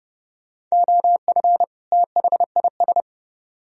Morse code, heard as a keyed tone, about 36 wpm, 700 Hz. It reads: OF THIS
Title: Speed-Racing - Sets of 2 Words 30wpm